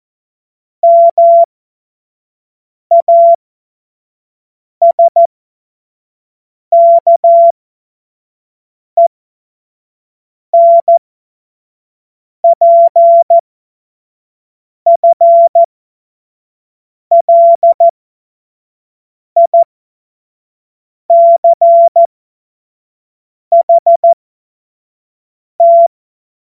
Morse code, heard as a keyed tone, about 14 wpm, spaced as for 5 wpm, 680 Hz. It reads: MASKENPFLICHT
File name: Morsekurs / Witze / Witz 10. August 2021, 25 F70